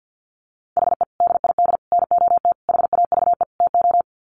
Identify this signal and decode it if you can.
Morse code, heard as a keyed tone, about 48 wpm, 700 Hz. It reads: HE DID NOT HAVE TO
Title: Sets of 5 Words 50wpm